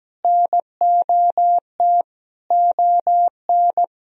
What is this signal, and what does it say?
Morse code, heard as a keyed tone, about 17 wpm, 700 Hz. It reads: NOT ON
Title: Sentences from Top 1000 Words - Review Part 4 - 17wpm